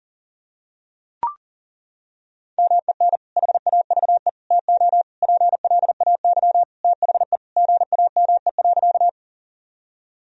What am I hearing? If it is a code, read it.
Morse code, heard as a keyed tone, about 40 wpm, 700 Hz. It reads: MEN HAVE TO PLAY THE GAME.
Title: Sentences from Top 1000 Words - Review Part 3 - 40wpm